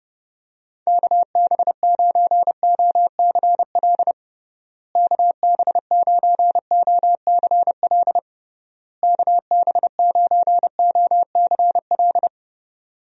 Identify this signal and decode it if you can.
Morse code, heard as a keyed tone, about 32 wpm, 700 Hz. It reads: KB9OCL KB9OCL KB9OCL
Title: USA Call Signs - 6 Characters - Repeated 3x - 30wpm